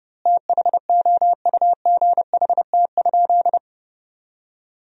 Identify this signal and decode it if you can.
Morse code, heard as a keyed tone, about 30 wpm, 700 Hz. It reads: THOUGHT?
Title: 3x Word-Spacing - Sentences from Top 300 Words - 30wpm